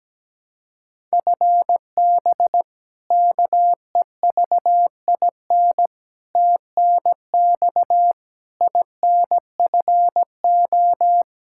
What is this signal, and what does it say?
Morse code, heard as a keyed tone, about 17 wpm, 700 Hz. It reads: FB KEVIN TNX INFO
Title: Practice QSO - Exchange 3 - More Info - 17wpm